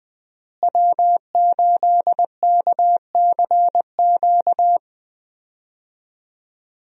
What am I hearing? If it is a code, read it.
Morse code, heard as a keyed tone, about 20 wpm, 700 Hz. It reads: W8KCQ
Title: USA Call Signs - 5 Characters - Repeated 2x - 20wpm